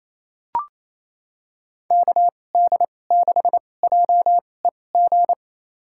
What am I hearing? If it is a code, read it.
Morse code, heard as a keyed tone, about 28 wpm, spaced as for 20 wpm, 700 Hz. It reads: KD6JEG
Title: USA Call Signs - 2x Character Spacing - 28wpm